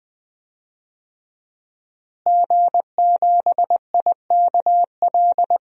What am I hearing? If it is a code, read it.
Morse code, heard as a keyed tone, about 20 wpm, 700 Hz. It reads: G7IKL
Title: DXCC-100 Call Signs - 5 Characters - Repeated 2x - 20wpm